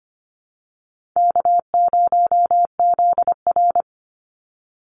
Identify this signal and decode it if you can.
Morse code, heard as a keyed tone, about 25 wpm, 700 Hz. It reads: K0ZR
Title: Easy Call Signs Repeated 3x - 25wpm